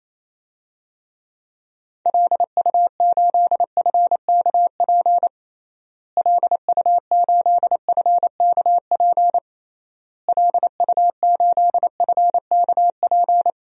Morse code, heard as a keyed tone, about 28 wpm, 700 Hz. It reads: LU8FKP LU8FKP LU8FKP
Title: DXCC-100 Call Signs - 6 Characters - Repeated 3x - 28wpm